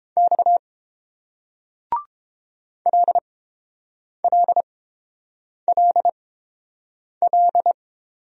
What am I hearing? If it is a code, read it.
Morse code, heard as a keyed tone, about 30 wpm, spaced as for 7 wpm, 700 Hz. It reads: X LLLL